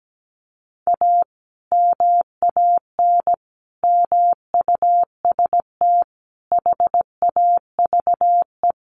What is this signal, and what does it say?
Morse code, heard as a keyed tone, about 17 wpm, 700 Hz. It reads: A MAN MUST HAVE
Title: Sentences from Top 1200 Words 17wpm